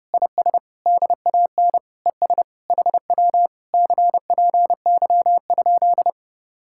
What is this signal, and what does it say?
Morse code, heard as a keyed tone, about 32 wpm, 700 Hz. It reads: IS DAN ES HW CPY?